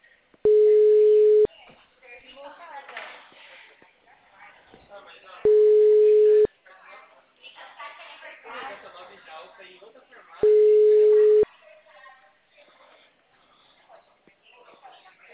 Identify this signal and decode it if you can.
Ringback tone (Europe, 425 Hz), with speech.